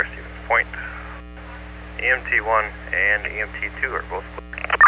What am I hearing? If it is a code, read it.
Emergency Alert System (SAME) tones, heard as harsh buzzing bursts, with speech. It is heard from 0.5 to 3.8 s.